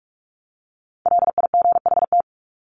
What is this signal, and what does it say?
Morse code, heard as a keyed tone, about 44 wpm, 700 Hz. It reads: LIGHT
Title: Top 300 Words Repeated 45wpm